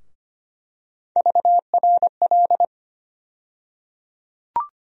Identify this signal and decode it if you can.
Morse code, heard as a keyed tone, about 25 wpm, 700 Hz. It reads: VRL
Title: USA Call Sign Postfixes - 3 Letters 25wpm